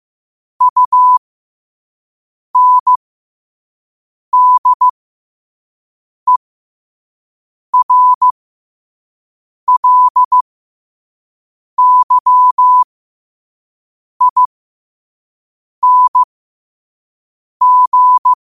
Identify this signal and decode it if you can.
Morse code, heard as a keyed tone, about 15 wpm, spaced as for 5 wpm, 1000 Hz. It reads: UNDERLYING
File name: 5wpm